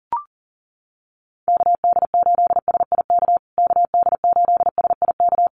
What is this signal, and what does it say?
Morse code, heard as a keyed tone, about 40 wpm, 700 Hz. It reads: KD8SIK KD8SIK
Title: USA Call Signs Repeated 3x 40wpm